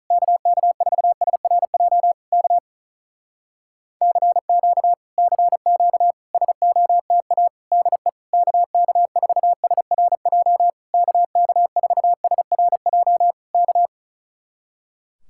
Morse code, heard as a keyed tone, about 42 wpm, 700 Hz. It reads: KK4SRJ K CQ CQ SOTA DE KK4SRJ KK4SRJ K
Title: Speed-Racing Common QSO Phrases Encore - 35wpm